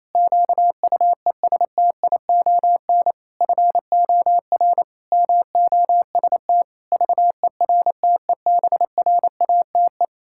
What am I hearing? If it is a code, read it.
Morse code, heard as a keyed tone, about 28 wpm, 700 Hz. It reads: QUESTION FOR MOST VERTEBRATE
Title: Pangram Challenge v2 - 28wpm